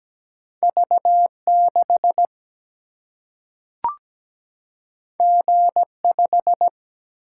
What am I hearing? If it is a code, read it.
Morse code, heard as a keyed tone, about 17 wpm, 700 Hz. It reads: V6 G5
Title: Single-Letter Single-Number 17wpm